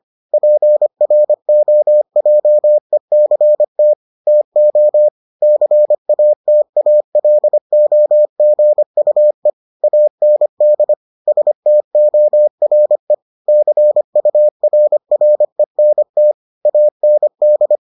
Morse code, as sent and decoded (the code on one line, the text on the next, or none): .--. .-. --- .--- . -.-. - / - --- / -.-. .- - .- .-.. --- --. ..- . / .- -. -.. / ... - --- .-. / -.-. ..- .-. .-. -. - / .- -. -..
PROJECT TO CATALOGUE AND STORE CURRENT AND